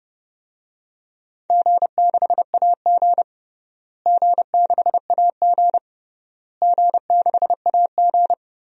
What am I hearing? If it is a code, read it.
Morse code, heard as a keyed tone, about 30 wpm, 700 Hz. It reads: G6AG G6AG G6AG
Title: DXCC-100 Call Signs - 4 Characters - Repeated 3x - 30wpm